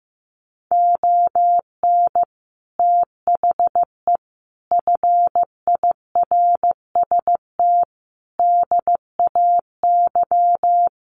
Morse code, heard as a keyed tone, about 15 wpm, 700 Hz. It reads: ON THE FIRST DAY
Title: Sets of 5 Words Encore 15wpm